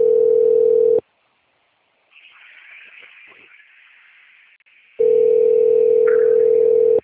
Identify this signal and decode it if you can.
Ringback tone (North America), 440 + 480 Hz.
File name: out- - - - - .23917